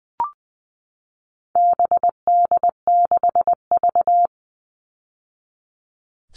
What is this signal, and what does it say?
Morse code, heard as a keyed tone, about 20 wpm, 700 Hz. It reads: BD6V
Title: All Cut Numbers - 4-Digit 20wpm